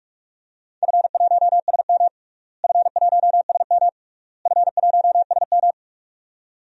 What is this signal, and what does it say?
Morse code, heard as a keyed tone, about 41 wpm, 700 Hz. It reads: F1SM F1SM F1SM
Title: DXCC-100 Call Signs - 4 Characters - Repeated 3x - 45wpm